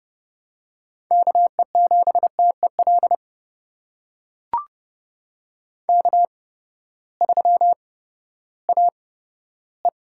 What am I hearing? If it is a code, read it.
Morse code, heard as a keyed tone, about 32 wpm, spaced as for 15 wpm, 700 Hz. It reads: KE7TEL K3AE
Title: USA Call Signs - 8x Character Spacing - 30wpm